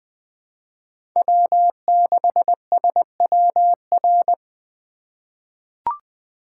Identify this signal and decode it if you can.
Morse code, heard as a keyed tone, about 20 wpm, 700 Hz. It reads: W6SWR
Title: USA Call Signs - 5 Characters - 2x Character Spacing - 20wpm